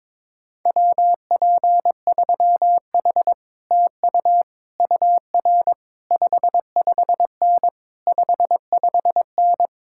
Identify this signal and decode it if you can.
Morse code, heard as a keyed tone, about 22 wpm, 700 Hz. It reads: WP3H TU UR 55N 55N
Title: Speed-Racing Common QSO Phrases Encore - 22wpm